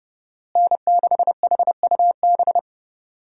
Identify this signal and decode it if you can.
Morse code, heard as a keyed tone, about 30 wpm, 700 Hz. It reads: N6HUB
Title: USA Call Signs - 5 Characters - Repeated 3x - 30wpm